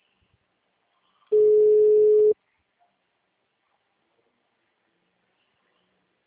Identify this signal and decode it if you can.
Ringback tone (Europe, 425 Hz).